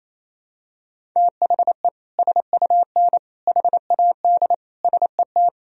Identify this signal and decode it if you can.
Morse code, heard as a keyed tone, about 28 wpm, 700 Hz. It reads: THE SUN HAD SET
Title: Sentences from Top 300 Words and Abbreviations 28wpm